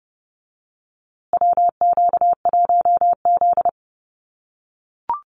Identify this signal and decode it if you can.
Morse code, heard as a keyed tone, about 30 wpm, 700 Hz. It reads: WQ1Z
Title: USA Call Signs Repeated 2x 30wpm